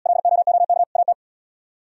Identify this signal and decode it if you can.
Morse code, heard as a keyed tone, about 70 wpm, 700 Hz. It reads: LOOK AT